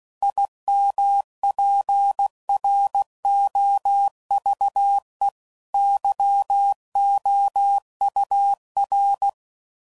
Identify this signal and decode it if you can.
Morse code, heard as a keyed tone, about 16 wpm, 800 Hz. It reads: IMPROVE YOUR
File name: Sayings / Quotes / A Smile